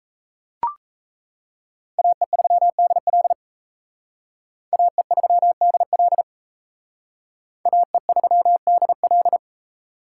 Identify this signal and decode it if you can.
Morse code, heard as a keyed tone, about 34 wpm, 700 Hz. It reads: AE3DL AE3DL AE3DL
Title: Speed Racing - USA 2x2 Call Signs 28wpm